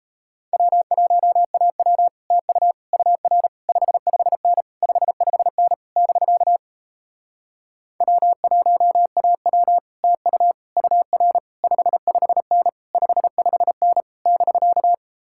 Morse code, heard as a keyed tone, about 38 wpm, 700 Hz. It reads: W1AW TU UR 55N 55N <BK> W1AW TU UR 55N 55N <BK>